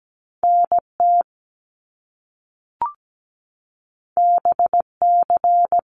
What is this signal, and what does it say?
Morse code, heard as a keyed tone, about 17 wpm, 700 Hz. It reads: NT BC
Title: Canadian Provinces and Territories 17wpm